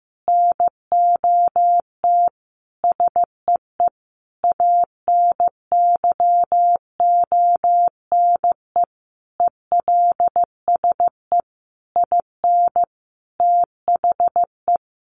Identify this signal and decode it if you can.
Morse code, heard as a keyed tone, about 15 wpm, 700 Hz. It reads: NOT SEE ANYONE ELSE IN THE
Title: Sentences from Top 900 Words 15wpm